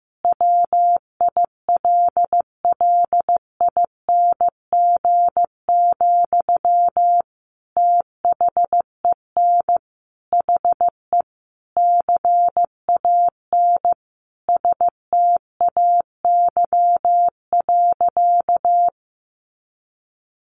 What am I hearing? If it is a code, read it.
Morse code, heard as a keyed tone, about 15 wpm, 700 Hz. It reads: WILLING, THEN HE CAN STAY.
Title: Sentences from Top 1000 Words 15wpm